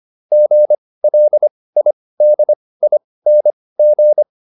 Morse code, heard as a keyed tone, about 25 wpm, spaced as for 18 wpm, 600 Hz. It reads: GLIDING